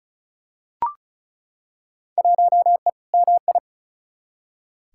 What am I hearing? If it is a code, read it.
Morse code, heard as a keyed tone, about 36 wpm, 700 Hz. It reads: 1E MI
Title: Field Day Exchanges - 35wpm